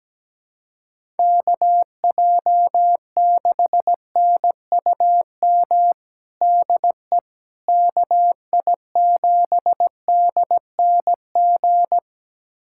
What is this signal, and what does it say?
Morse code, heard as a keyed tone, about 17 wpm, 700 Hz. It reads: KJ6NUM DE KI7DNG
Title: USA Callsign DE Callsign - 6 Characters - 17wpm